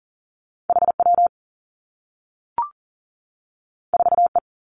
Morse code, heard as a keyed tone, about 42 wpm, 700 Hz. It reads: H W 4 E
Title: Two Challenging Letter-Number Combinations 40wpm